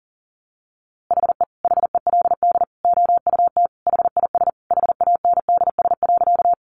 Morse code, heard as a keyed tone, about 40 wpm, 700 Hz. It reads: HE HELD OUT HIS HANDS.
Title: A Sound Of Thunder 40wpm